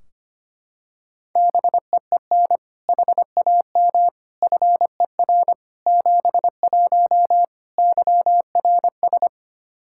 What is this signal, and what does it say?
Morse code, heard as a keyed tone, about 25 wpm, 700 Hz. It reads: BEEN HAM FER 71 YRS
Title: Practice QSO - Exchange 3 - More Info - 25wpm